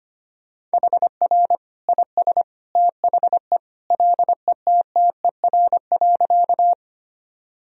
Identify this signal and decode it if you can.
Morse code, heard as a keyed tone, about 25 wpm, 700 Hz. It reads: HR IS THE LETTER.
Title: Sentences from Top 300 Words and Abbreviations 25wpm